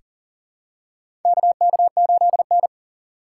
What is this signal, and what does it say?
Morse code, heard as a keyed tone, about 37 wpm, 700 Hz. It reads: KK8N
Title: USA Call Signs - 4 Characters - 4x Character Spacing - 40wpm